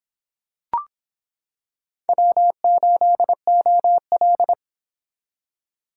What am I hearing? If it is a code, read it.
Morse code, heard as a keyed tone, about 26 wpm, 700 Hz. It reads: W8OL